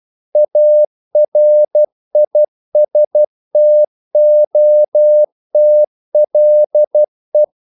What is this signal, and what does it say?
Morse code, heard as a keyed tone, about 12 wpm, 600 Hz. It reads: ARISTOTLE
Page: English quotations in CW (telegraphy - morse)